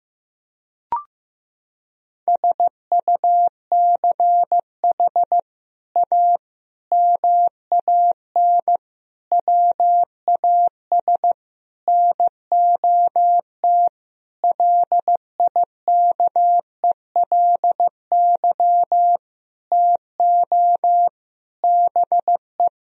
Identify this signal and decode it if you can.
Morse code, heard as a keyed tone, about 16 wpm, 700 Hz. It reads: SUCH A MAN WAS NOT LIKELY TO BE
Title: Sentences from Top 1000 Words - Review Part 4 - 15wpm